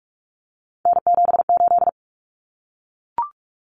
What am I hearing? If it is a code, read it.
Morse code, heard as a keyed tone, about 44 wpm, 700 Hz. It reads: N78
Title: Common Cut Numbers - 3-Digit 45wpm